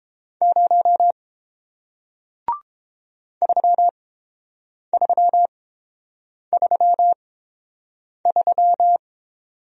Morse code, heard as a keyed tone, about 30 wpm, spaced as for 7 wpm, 700 Hz. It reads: 0 3333